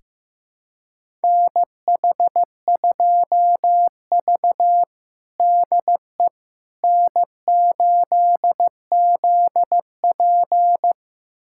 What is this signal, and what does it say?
Morse code, heard as a keyed tone, about 16 wpm, 700 Hz. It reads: NH2V DE N8ZP
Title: USA Callsign DE Callsign - 4 Characters - 15wpm